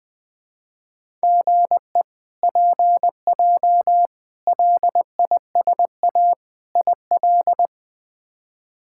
Morse code, heard as a keyed tone, about 21 wpm, 700 Hz. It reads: GE PJ LISA IL
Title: K1USN SST QSO Practice - 20wpm